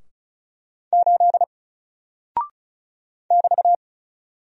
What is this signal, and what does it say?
Morse code, heard as a keyed tone, about 36 wpm, 700 Hz. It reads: 8 <BT>
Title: Single Character (Letter:Number:Punctuation:Prosign) Alt 35wpm